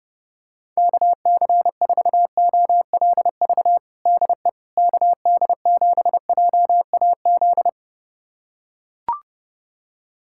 Morse code, heard as a keyed tone, about 30 wpm, 700 Hz. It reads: KC4OLV DE KD7JAZ
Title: USA Callsign DE Callsign - 6 Characters - 30wpm